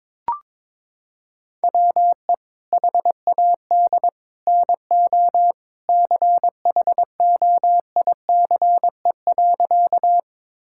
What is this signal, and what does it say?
Morse code, heard as a keyed tone, about 21 wpm, 700 Hz. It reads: WE HAD NO CHOICE.
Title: Sentences from Top 1000 Words - Review Part 6 - 22wpm